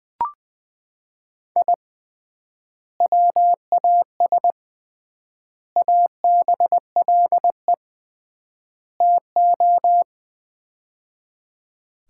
Morse code, heard as a keyed tone, about 20 wpm, 700 Hz. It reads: I WAS ABLE TO
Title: 3x Word-Spacing - Sets of 4 Words - 20wpm